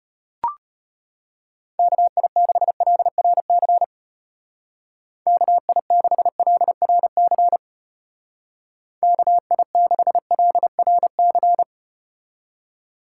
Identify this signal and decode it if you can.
Morse code, heard as a keyed tone, about 34 wpm, 700 Hz. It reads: KI6LRC KI6LRC KI6LRC